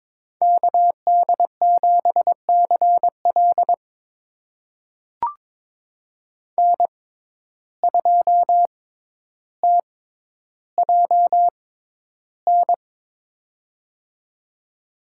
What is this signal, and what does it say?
Morse code, heard as a keyed tone, about 22 wpm, spaced as for 11 wpm, 700 Hz. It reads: KD7CL N2TJN
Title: USA Call Signs - 5 Characters - 6x Character Spacing - 22wpm